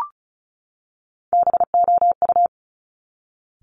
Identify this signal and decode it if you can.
Morse code, heard as a keyed tone, about 36 wpm, 700 Hz. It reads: BOU
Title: Top Three Letter Combinations in Words 35wpm